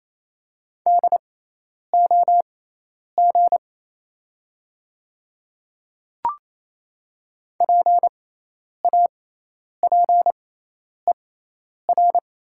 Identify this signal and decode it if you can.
Morse code, heard as a keyed tone, about 28 wpm, spaced as for 10 wpm, 700 Hz. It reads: DOG PAPER
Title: 6x Character Spacing - Top 500 Words - 28wpm